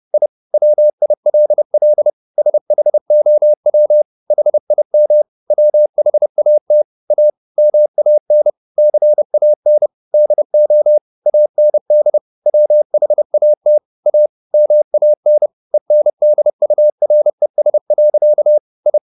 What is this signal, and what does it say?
Morse code, heard as a keyed tone, about 30 wpm, 600 Hz. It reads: I WILL SHOW HIM WHAT A MAN CAN DO AND WHAT A MAN ENDURES. I
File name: the old man and the sea / the old man and the sea 1 30wpm